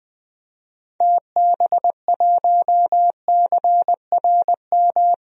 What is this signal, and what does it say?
Morse code, heard as a keyed tone, about 20 wpm, 700 Hz. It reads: TB1CRM
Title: DXCC-100 Call Signs - 6 Characters - Repeated 3x - 20wpm